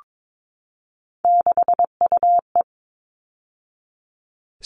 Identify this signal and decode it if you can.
Morse code, heard as a keyed tone, about 22 wpm, 700 Hz. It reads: 6UE
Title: All Cut Numbers - 3-Digit 22wpm